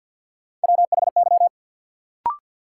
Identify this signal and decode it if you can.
Morse code, heard as a keyed tone, about 53 wpm, 700 Hz. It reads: WHY
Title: Sets of 3 Words - New Word Drill - 50wpm